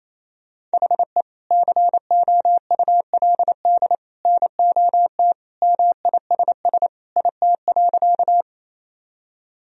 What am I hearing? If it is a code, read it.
Morse code, heard as a keyed tone, about 27 wpm, 700 Hz. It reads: HE COULD NOT MISS IT.
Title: Sentences from Top 1000 Words - Review Part 3 - 28wpm